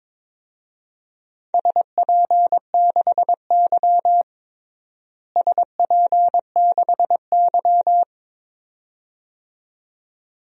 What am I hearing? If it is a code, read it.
Morse code, heard as a keyed tone, about 22 wpm, 700 Hz. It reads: SP6Y SP6Y
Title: DXCC-100 Call Signs - 4 Characters - Repeated 2x - 22wpm